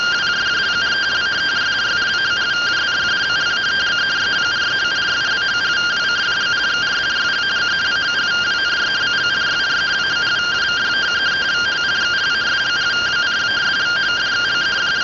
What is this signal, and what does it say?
Two-tone teleprinter signal, about 170 Hz shift.